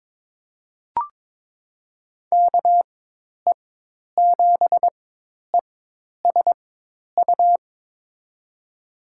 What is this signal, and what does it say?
Morse code, heard as a keyed tone, about 21 wpm, spaced as for 10 wpm, 700 Hz. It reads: KE7ESU
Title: USA Call Signs - 6 Characters - 4x Character Spacing - 22wpm